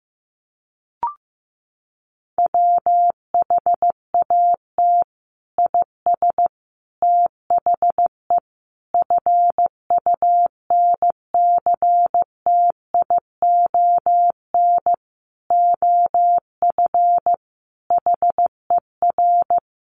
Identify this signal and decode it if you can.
Morse code, heard as a keyed tone, about 15 wpm, 700 Hz. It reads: WHAT IS THE FUNCTION OF HER
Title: Sentences from Top 1100 Words 15wpm